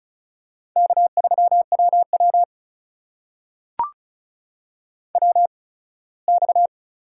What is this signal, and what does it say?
Morse code, heard as a keyed tone, about 34 wpm, spaced as for 21 wpm, 700 Hz. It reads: K3WW WX
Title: USA Call Signs - 4 Characters - 8x Character Spacing - 35wpm